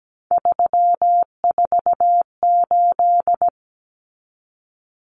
Morse code, heard as a keyed tone, about 17 wpm, 700 Hz. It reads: 348
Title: Common Cut Numbers - 3-Digit 17wpm